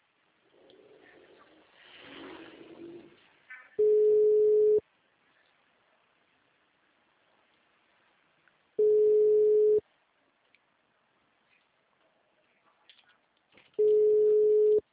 Ringback tone (Europe, 425 Hz).
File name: out- -2029- - - .5742